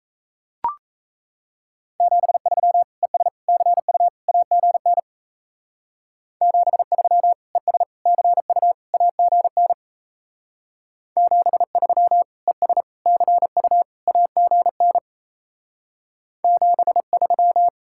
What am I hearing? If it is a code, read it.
Morse code, heard as a keyed tone, about 38 wpm, 700 Hz. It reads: 73 ES CU AGN 73 ES CU AGN 73 ES CU AGN 73